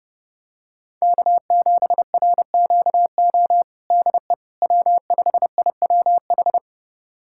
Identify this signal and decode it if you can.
Morse code, heard as a keyed tone, about 30 wpm, 700 Hz. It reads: K7RQO DE W5IWH
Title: USA Callsign DE Callsign - 5 Characters - 30wpm